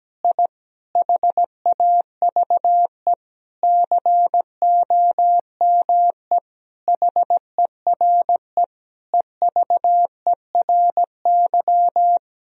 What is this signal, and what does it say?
Morse code, heard as a keyed tone, about 17 wpm, 700 Hz. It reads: I HAVE COME HERE EVERY
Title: Sentences from Top 1000 Words - Review Part 1 - 17wpm